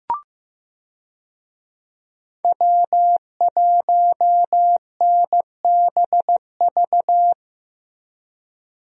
Morse code, heard as a keyed tone, about 15 wpm, 700 Hz. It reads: W1NBV
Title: USA Call Signs - 5 Characters - Repeated 2x - 15wpm